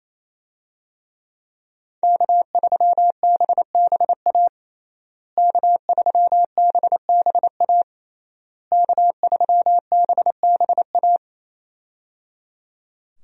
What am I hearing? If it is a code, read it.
Morse code, heard as a keyed tone, about 27 wpm, 700 Hz. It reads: K3BBA K3BBA K3BBA
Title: USA Call Signs - 5 Characters - Repeated 3x - 28wpm